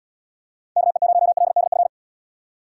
Morse code, heard as a keyed tone, about 78 wpm, 700 Hz. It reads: KE0LDF